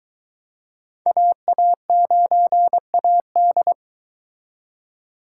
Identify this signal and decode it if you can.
Morse code, heard as a keyed tone, about 23 wpm, 700 Hz. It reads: AA9AD